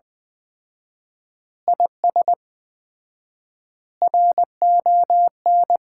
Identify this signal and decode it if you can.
Morse code, heard as a keyed tone, about 20 wpm, 700 Hz. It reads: IS RON
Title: Common QSO Phrases - 4x Word-Spacing - 20wpm